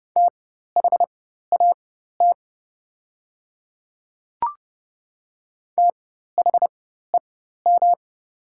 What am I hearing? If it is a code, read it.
Morse code, heard as a keyed tone, about 30 wpm, spaced as for 14 wpm, 700 Hz. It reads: THAT THEM
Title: 4x Character Spacing - Top 100 Words - 30wpm